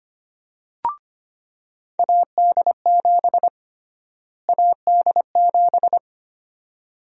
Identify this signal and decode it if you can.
Morse code, heard as a keyed tone, about 25 wpm, 700 Hz. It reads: AD7 AD7
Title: USA Call Sign Prefixes - Repeated 2x - 25wpm